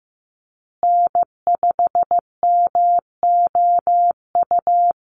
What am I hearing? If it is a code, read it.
Morse code, heard as a keyed tone, about 15 wpm, 700 Hz. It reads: N5MOU
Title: USA Call Signs Repeated 3x 15wpm